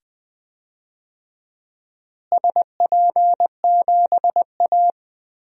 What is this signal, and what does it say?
Morse code, heard as a keyed tone, about 20 wpm, 700 Hz. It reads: SP7A
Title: DXCC-100 Call Signs - 4 Characters - Repeated 2x - 20wpm